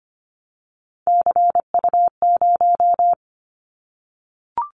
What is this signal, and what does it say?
Morse code, heard as a keyed tone, about 25 wpm, 700 Hz. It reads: CU0
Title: Callsign Prefixes for Top 50 DXCC - 25wpm